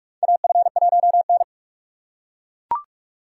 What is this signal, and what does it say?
Morse code, heard as a keyed tone, about 41 wpm, 700 Hz. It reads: AF1N
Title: USA Call Signs - 4 Characters - 4x Character Spacing - 45wpm